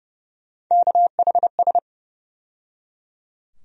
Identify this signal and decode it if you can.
Morse code, heard as a keyed tone, about 30 wpm, 700 Hz. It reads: KHS
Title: USA Call Sign Postfixes - 3 Letters 30wpm